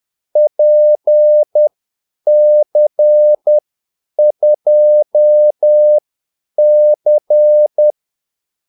Morse code, heard as a keyed tone, about 10 wpm, spaced as for 8 wpm, 600 Hz. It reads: PC2C
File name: Callsigns-10-8wpm